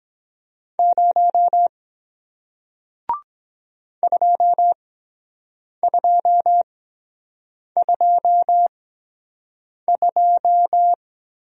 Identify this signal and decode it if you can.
Morse code, heard as a keyed tone, about 23 wpm, spaced as for 7 wpm, 700 Hz. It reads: 0 2222